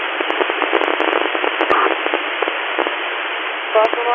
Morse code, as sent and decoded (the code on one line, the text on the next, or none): none